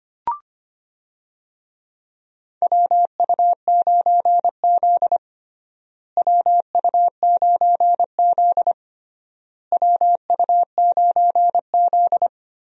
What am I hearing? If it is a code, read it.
Morse code, heard as a keyed tone, about 25 wpm, 700 Hz. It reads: WU9Z WU9Z WU9Z
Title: USA Call Signs - 4 Characters - Repeated 3x - 25wpm